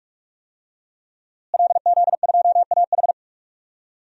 Morse code, heard as a keyed tone, about 46 wpm, 700 Hz. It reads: LZ2AH
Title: DXCC-100 Call Signs - 5 Characters - Repeated 2x - 45wpm